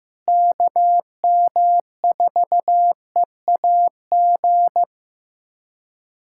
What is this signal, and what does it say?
Morse code, heard as a keyed tone, about 15 wpm, 700 Hz. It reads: KM4EAG
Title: USA Callsign DE Callsign - 6 Characters - 15wpm